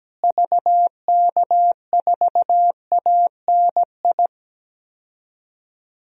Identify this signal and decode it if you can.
Morse code, heard as a keyed tone, about 17 wpm, 700 Hz. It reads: VK4ANI
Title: DXCC-100 Call Signs - 6 Characters - Repeated 2x - 17wpm